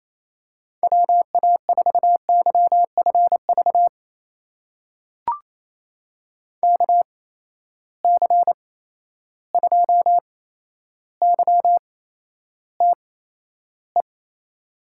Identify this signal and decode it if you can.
Morse code, heard as a keyed tone, about 28 wpm, spaced as for 12 wpm, 700 Hz. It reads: WA4YFV KC2YTE
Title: USA Call Signs - 8x Character Spacing - 28wpm